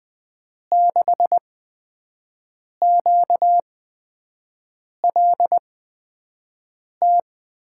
Morse code, heard as a keyed tone, about 20 wpm, spaced as for 5 wpm, 700 Hz. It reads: 6QLT
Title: USA Call Signs - 6 Characters - 8x Character Spacing - 20wpm